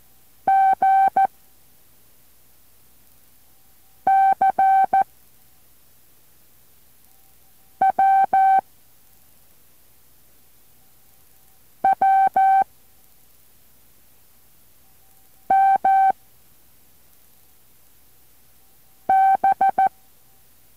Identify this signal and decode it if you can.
Morse code, heard as a keyed tone, about 14 wpm, 770 Hz. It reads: G C W W M B